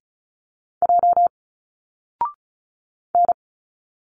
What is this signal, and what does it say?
Morse code, heard as a keyed tone, about 34 wpm, 700 Hz. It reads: J N
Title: Single Letters 35wpm